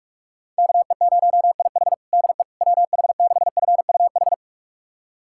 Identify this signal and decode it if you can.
Morse code, heard as a keyed tone, about 46 wpm, 690 Hz. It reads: KE0IH DE WH6FUH
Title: USA Callsign DE Callsign 45wpm